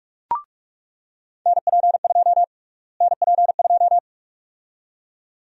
Morse code, heard as a keyed tone, about 47 wpm, 700 Hz. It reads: NP2 NP2
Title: USA Call Sign Prefixes - Repeated 2x - 45wpm